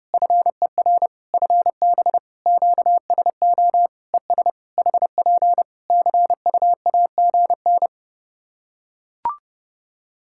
Morse code, heard as a keyed tone, about 30 wpm, 700 Hz. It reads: FER FB QSO ES HP CUAGN
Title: Practice QSO - Exchange 4 - Ending - 30wpm